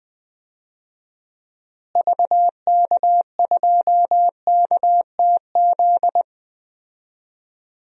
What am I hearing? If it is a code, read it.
Morse code, heard as a keyed tone, about 20 wpm, 700 Hz. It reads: VK2KTZ
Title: DXCC-100 Call Signs - 6 Characters - Repeated 2x - 20wpm